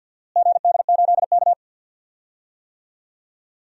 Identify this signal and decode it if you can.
Morse code, heard as a keyed tone, about 49 wpm, 700 Hz. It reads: GD8K
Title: DXCC-100 Call Signs - 4 Characters - Repeated 2x - 50wpm